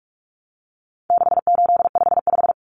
Morse code, heard as a keyed tone, about 44 wpm, 700 Hz. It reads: 6855
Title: Common Cut Numbers - 4-Digit 45wpm